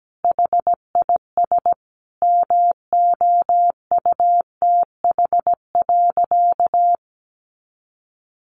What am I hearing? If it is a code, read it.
Morse code, heard as a keyed tone, about 17 wpm, 700 Hz. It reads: HIS MOUTH.
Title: A Sound Of Thunder 17wpm